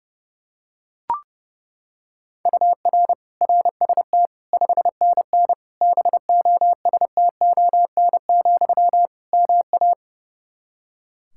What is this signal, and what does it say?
Morse code, heard as a keyed tone, about 32 wpm, 700 Hz. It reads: UR RST 5NN BOSTON, MA
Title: Practice QSO - Exchange 1 - Beginning - 30wpm